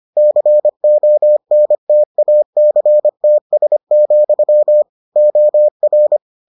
Morse code, heard as a keyed tone, about 25 wpm, 600 Hz. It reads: CONTACTS, OR